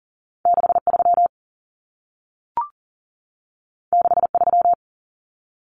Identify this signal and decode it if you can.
Morse code, heard as a keyed tone, about 40 wpm, 700 Hz. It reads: 63 63
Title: Numbers - 2-Digit 40wpm